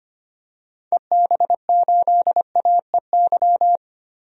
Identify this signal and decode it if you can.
Morse code, heard as a keyed tone, about 25 wpm, 700 Hz. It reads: EB8AEY
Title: DXCC-100 Call Signs - 6 Characters - 25wpm